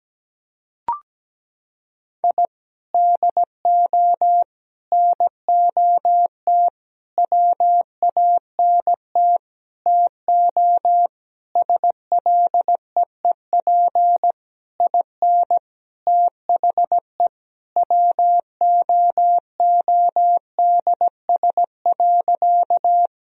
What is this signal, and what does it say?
Morse code, heard as a keyed tone, about 17 wpm, 700 Hz. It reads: I DO NOT WANT TO SLEEP IN THE WOODS.
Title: Sentences from Top 1000 Words - Review Part 2 - 17wpm